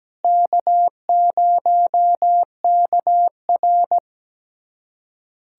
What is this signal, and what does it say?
Morse code, heard as a keyed tone, about 17 wpm, 700 Hz. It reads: K0KR
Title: USA Callsign DE Callsign - 4 Characters - 17wpm